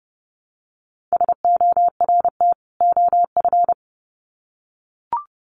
Morse code, heard as a keyed tone, about 30 wpm, 700 Hz. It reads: SORT OF
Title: Sets of 2 Words 30wpm